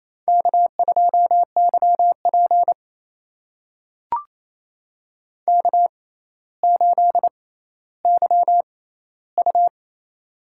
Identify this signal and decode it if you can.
Morse code, heard as a keyed tone, about 28 wpm, spaced as for 14 wpm, 700 Hz. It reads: K2YP K8YU
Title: USA Call Signs - 4 Characters - 6x Character Spacing - 28wpm